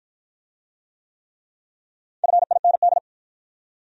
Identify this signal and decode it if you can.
Morse code, heard as a keyed tone, about 53 wpm, 700 Hz. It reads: FIND